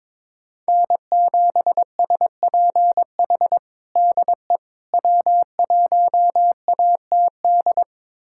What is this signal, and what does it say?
Morse code, heard as a keyed tone, about 23 wpm, 700 Hz. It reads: N7SPH DE W1ATD
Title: USA Callsign DE Callsign - 5 Characters - 22wpm